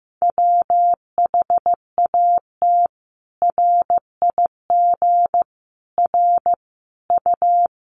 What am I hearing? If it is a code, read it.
Morse code, heard as a keyed tone, about 14 wpm, 700 Hz. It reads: WHAT RIG R U
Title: Common QSO Phrases 15wpm